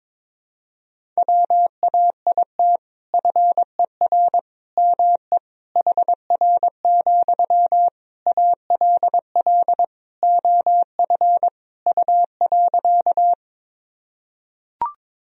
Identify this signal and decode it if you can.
Morse code, heard as a keyed tone, about 22 wpm, 700 Hz. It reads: WAIT FER ME HR, ALL OF U.
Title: Sentences from Top 500 Words and Abbreviations 22wpm